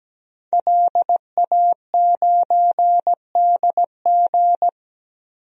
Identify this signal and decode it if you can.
Morse code, heard as a keyed tone, about 17 wpm, 700 Hz. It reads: LA9DG
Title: DXCC-100 Call Signs - 5 Characters - 17wpm